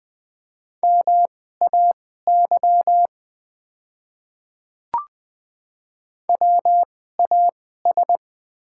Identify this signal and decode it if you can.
Morse code, heard as a keyed tone, about 20 wpm, spaced as for 14 wpm, 690 Hz. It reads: MAY WAS
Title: 2x Character Spacing - Top 100 Words - 20wpm